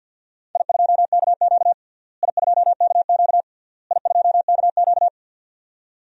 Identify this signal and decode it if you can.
Morse code, heard as a keyed tone, about 50 wpm, 700 Hz. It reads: I2KQ I2KQ I2KQ
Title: DXCC-100 Call Signs - 4 Characters - Repeated 3x - 50wpm